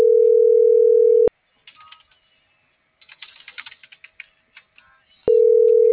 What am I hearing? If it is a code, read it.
Ringback tone (North America), 440 + 480 Hz.